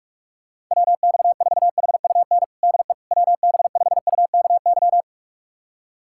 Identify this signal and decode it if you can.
Morse code, heard as a keyed tone, about 42 wpm, 700 Hz. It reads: WX4HUN DE WB5UKY
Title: USA Callsign DE Callsign - 6 Characters - 45wpm